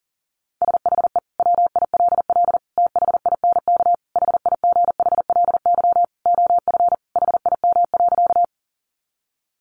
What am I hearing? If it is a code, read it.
Morse code, heard as a keyed tone, about 40 wpm, 700 Hz. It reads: SHE WILL THINK HIGHLY OF HIM.
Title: Sentences from Top 1100 Words 40wpm